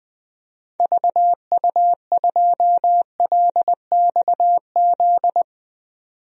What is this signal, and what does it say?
Morse code, heard as a keyed tone, about 20 wpm, 700 Hz. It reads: VU2LXZ
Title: DXCC-100 Call Signs - 6 Characters - 20wpm